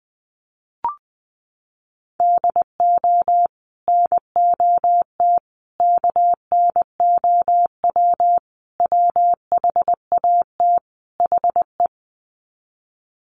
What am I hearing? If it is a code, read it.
Morse code, heard as a keyed tone, about 20 wpm, 700 Hz. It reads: DO NOT KNOW WHAT HE
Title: Sets of 5 Words Encore2 20wpm